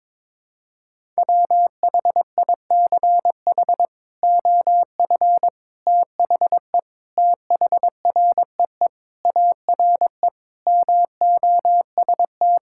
Morse code, heard as a keyed tone, about 22 wpm, 700 Hz. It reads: WHICH OF THE THREE ARE MOST
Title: Sentences from Top 1000 Words - Review Part 5 - 22wpm